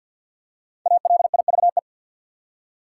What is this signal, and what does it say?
Morse code, heard as a keyed tone, about 49 wpm, 700 Hz. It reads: ALIVE
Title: Five Letter Words - 50wpm